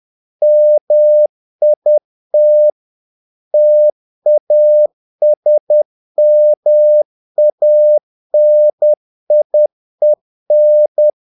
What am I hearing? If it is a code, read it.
Morse code, heard as a keyed tone, about 10 wpm, 600 Hz. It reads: MIT TASMANIEN
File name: dk5ke / Humor-50BpM